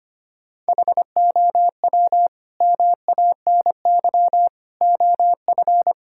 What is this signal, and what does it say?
Morse code, heard as a keyed tone, about 25 wpm, 700 Hz. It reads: HOW MANY OF
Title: Sentences from Top 1000 Words - Review Part 6 - 25wpm